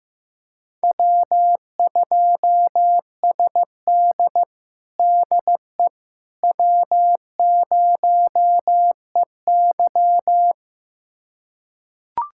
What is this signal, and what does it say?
Morse code, heard as a keyed tone, about 15 wpm, 700 Hz. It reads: W2SD DE W0EY
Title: USA Callsign DE Callsign - 4 Characters - 15wpm